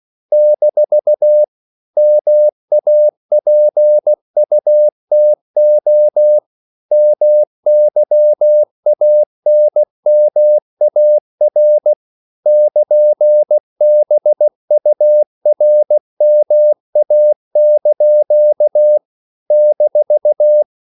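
Morse code, heard as a keyed tone, about 16 wpm, 600 Hz. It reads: - MAPUTO MYANMAR <KN>BURMA) -
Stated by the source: CW practice of national capitals at 16 words per minute, from ON6ZQ.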